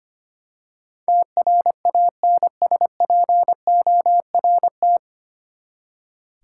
Morse code, heard as a keyed tone, about 25 wpm, 700 Hz. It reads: TRANSPORT
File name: Nine Letter Words - 25wpm